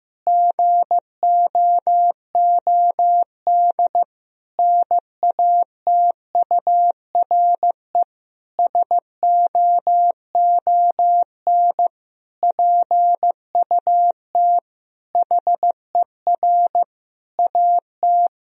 Morse code, heard as a keyed tone, about 15 wpm, 700 Hz. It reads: GOOD NATURE SOON PUT HER AT
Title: Sentences from Top 1000 Words - Review Part 1 - 15wpm